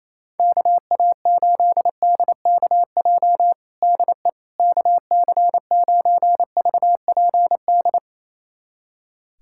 Morse code, heard as a keyed tone, about 27 wpm, 700 Hz. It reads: KA8DKJ DE KC9VPD
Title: USA Callsign DE Callsign - 6 Characters - 28wpm